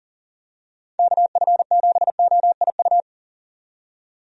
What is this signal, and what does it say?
Morse code, heard as a keyed tone, about 40 wpm, 700 Hz. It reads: KF7OIU
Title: USA Call Signs - 6 Characters - 40wpm